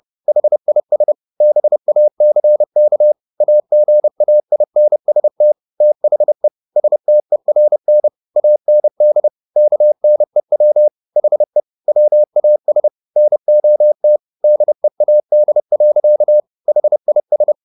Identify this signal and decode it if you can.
Morse code, heard as a keyed tone, about 30 wpm, 600 Hz. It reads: HIS BACK AGAINST THE STERN AND KNEW HE WAS NOT DEAD. HIS